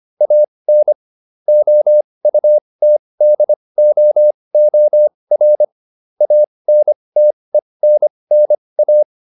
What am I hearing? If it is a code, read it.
Morse code, heard as a keyed tone, about 25 wpm, spaced as for 20 wpm, 600 Hz. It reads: AN OUTDOOR ANTENNA